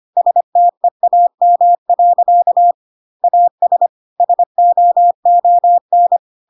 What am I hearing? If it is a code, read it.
Morse code, heard as a keyed tone, about 25 wpm, 700 Hz. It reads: STEAM. AS SOON
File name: Pinocchio / Pinocchio ch3 25 25 ik1hge